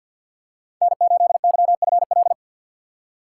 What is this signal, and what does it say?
Morse code, heard as a keyed tone, about 53 wpm, 700 Hz. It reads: N8YFL
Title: USA Call Signs - 5 Characters - 6x Character Spacing - 50wpm